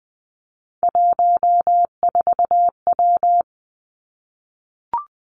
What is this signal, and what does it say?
Morse code, heard as a keyed tone, about 19 wpm, 700 Hz. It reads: 14W
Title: Three Challenging Letter-Number Combinations 20wpm